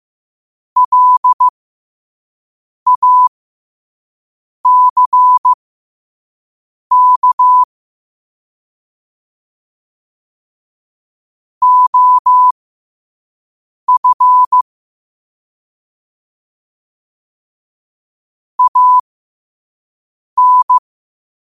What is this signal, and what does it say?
Morse code, heard as a keyed tone, about 15 wpm, spaced as for 5 wpm, 1000 Hz. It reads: LACK OF AN
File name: qotdfiles / 5wpm